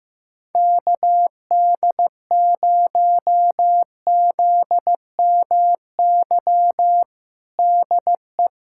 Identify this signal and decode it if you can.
Morse code, heard as a keyed tone, about 15 wpm, 700 Hz. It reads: KD0ZMY DE
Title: USA Callsign DE Callsign - 6 Characters - 15wpm